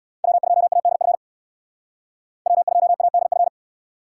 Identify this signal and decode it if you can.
Morse code, heard as a keyed tone, about 64 wpm, 700 Hz. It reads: W2ADF W2ADF